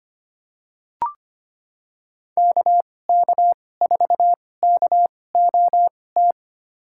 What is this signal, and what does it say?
Morse code, heard as a keyed tone, about 25 wpm, spaced as for 18 wpm, 700 Hz. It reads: KK4KOT
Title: USA Call Signs - 2x Character Spacing - 25wpm